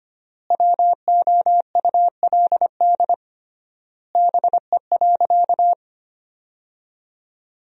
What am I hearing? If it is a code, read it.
Morse code, heard as a keyed tone, about 25 wpm, 700 Hz. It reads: WOULD BE.
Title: 3x Word-Spacing - Sentences from Top 200 Words - 25wpm